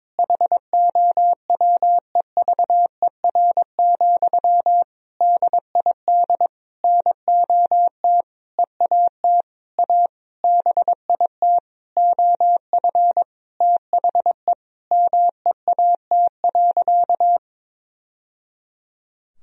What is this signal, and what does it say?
Morse code, heard as a keyed tone, about 23 wpm, 700 Hz. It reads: HOWEVER, DID NOT EAT A BIT OF THE MEAT.
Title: Sentences from Top 1000 Words - Review Part 3 - 22wpm